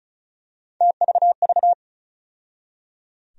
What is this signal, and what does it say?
Morse code, heard as a keyed tone, about 36 wpm, 700 Hz. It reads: TVV
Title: USA Call Sign Postfixes - 3 Letters 35wpm